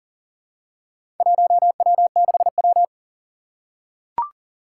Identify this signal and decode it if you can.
Morse code, heard as a keyed tone, about 42 wpm, 700 Hz. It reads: 1W6W
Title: Four Challenging Letter-Number Combinations 40wpm